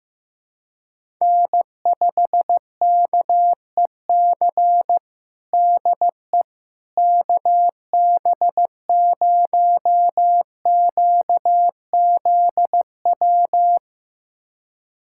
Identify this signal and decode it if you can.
Morse code, heard as a keyed tone, about 15 wpm, 700 Hz. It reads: N5KEC DE KB0QZW
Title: USA Callsign DE Callsign 15wpm